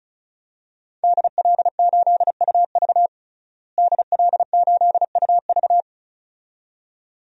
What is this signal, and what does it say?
Morse code, heard as a keyed tone, about 36 wpm, 700 Hz. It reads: DL8UV DL8UV
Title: DXCC-100 Call Signs - 5 Characters - Repeated 2x - 35wpm